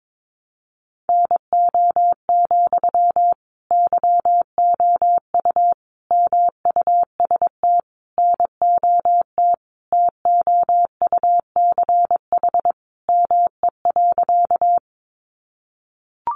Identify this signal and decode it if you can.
Morse code, heard as a keyed tone, about 22 wpm, 700 Hz. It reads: NO, YOU MUST NOT TOUCH ME.
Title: Sentences from Top 1200 Words 22wpm